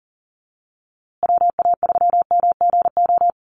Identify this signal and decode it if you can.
Morse code, heard as a keyed tone, about 40 wpm, 700 Hz. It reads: WA3MGO
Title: USA Call Signs Repeated 3x 40wpm